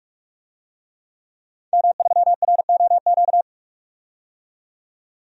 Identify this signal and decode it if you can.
Morse code, heard as a keyed tone, about 46 wpm, 700 Hz. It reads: M3ROQ
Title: DXCC-100 Call Signs - 5 Characters - 45wpm